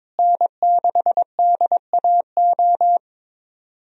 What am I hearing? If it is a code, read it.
Morse code, heard as a keyed tone, about 22 wpm, 700 Hz. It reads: N6DAO
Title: USA Call Signs - 5 Characters - Repeated 2x - 22wpm